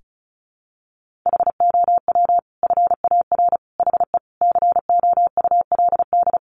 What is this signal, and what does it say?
Morse code, heard as a keyed tone, about 34 wpm, 700 Hz. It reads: HOW FAR HE COULD
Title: Sentences from Top 1500 Words 35wpm